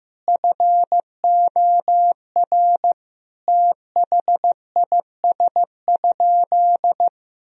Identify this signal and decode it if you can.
Morse code, heard as a keyed tone, about 15 wpm, 700 Hz. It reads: FOR THIS?
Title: Sentences from Top 1000 Words - Review Part 4 - 15wpm